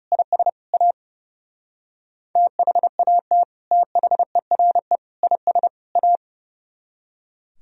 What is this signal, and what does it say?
Morse code, heard as a keyed tone, about 40 wpm, 700 Hz. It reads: IS A THAT THERE IS A